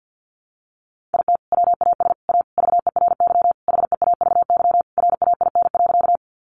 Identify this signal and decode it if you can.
Morse code, heard as a keyed tone, about 50 wpm, 700 Hz. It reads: IT WAS A VERY HEAVY RAIN.
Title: Sentences from Top 1100 Words 50wpm